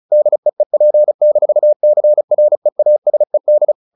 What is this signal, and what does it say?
Morse code, heard as a keyed tone, about 34 wpm, 600 Hz. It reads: DEEP-CREASED
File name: the old man and the sea / the old man and the sea 1 35wpm